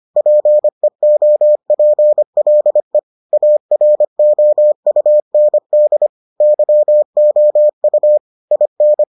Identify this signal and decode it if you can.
Morse code, heard as a keyed tone, about 25 wpm, 600 Hz. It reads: PEOPLE AROUND YOU IN